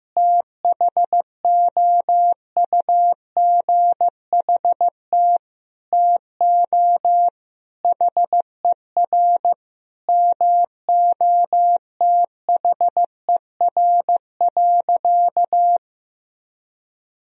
Morse code, heard as a keyed tone, about 14 wpm, 700 Hz. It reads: THOUGHT TO HER MOTHER.
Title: Sentences from Top 1000 Words - Review Part 2 - 15wpm